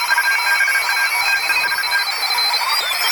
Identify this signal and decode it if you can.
Morse code, heard as a keyed tone, about 33 wpm, 1600 Hz, with noise. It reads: UBA TEST
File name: attachments / 7mHz test